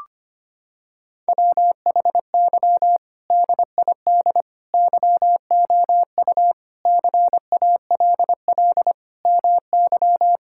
Morse code, heard as a keyed tone, about 25 wpm, 700 Hz. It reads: WHY DID YOU CALL MY
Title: Sentences from Top 1000 Words - Review Part 6 - 25wpm